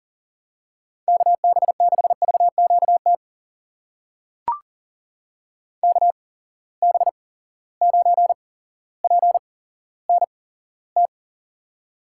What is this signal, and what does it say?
Morse code, heard as a keyed tone, about 40 wpm, spaced as for 17 wpm, 700 Hz. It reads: KB6VQT KB9PNT
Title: USA Call Signs - 6 Characters - 8x Character Spacing - 40wpm